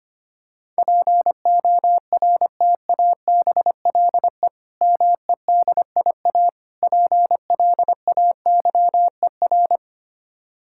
Morse code, heard as a keyed tone, about 25 wpm, 700 Hz. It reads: PORTABLE MEDIA PLAYER
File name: practice / 25wpm / Difficult Words - Speed-Racing with Hints - 25wpm